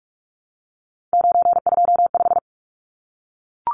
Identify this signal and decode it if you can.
Morse code, heard as a keyed tone, about 44 wpm, 700 Hz. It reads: 925
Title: Numbers - 3-Digit 45wpm